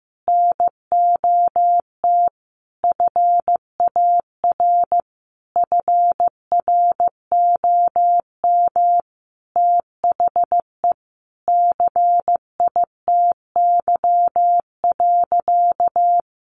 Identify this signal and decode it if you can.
Morse code, heard as a keyed tone, about 14 wpm, 700 Hz. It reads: NOT FAR FROM THE CITY.
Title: Sentences from Top 1200 Words 15wpm